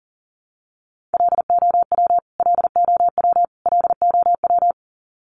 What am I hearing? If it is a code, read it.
Morse code, heard as a keyed tone, about 40 wpm, 700 Hz. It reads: LOW LOW LOW
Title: Top 200 Words Repeated 40wpm